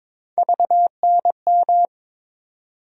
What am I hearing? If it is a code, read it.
Morse code, heard as a keyed tone, about 21 wpm, 700 Hz. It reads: VNM
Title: USA Call Sign Postfixes - 3 Letters 22wpm